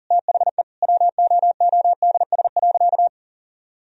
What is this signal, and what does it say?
Morse code, heard as a keyed tone, about 40 wpm, 700 Hz. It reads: THE WOODS.
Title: Sentences from Top 1000 Words - Review Part 2 - 40wpm